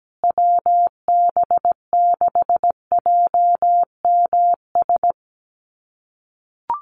Morse code, heard as a keyed tone, about 17 wpm, 700 Hz. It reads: WB6JMS
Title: USA Call Signs Repeated 2x 17wpm